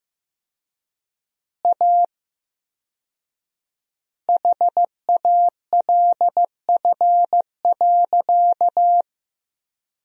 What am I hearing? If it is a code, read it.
Morse code, heard as a keyed tone, about 15 wpm, 700 Hz. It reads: A HALF.
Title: 4x Word-Spacing - Sentences from Top 400 Words - 15wpm